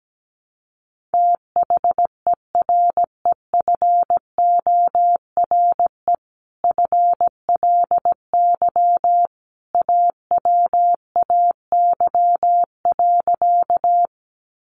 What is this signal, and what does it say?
Morse code, heard as a keyed tone, about 17 wpm, 700 Hz. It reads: THEREFORE FLY AWAY.
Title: Sentences from Top 800 Words 17wpm